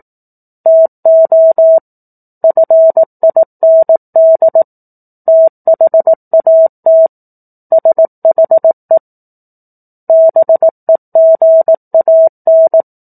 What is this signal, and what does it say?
Morse code, heard as a keyed tone, about 18 wpm, 650 Hz. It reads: TO FIND THAT SHE BEGAN
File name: 04-Chapter-04 18 WPM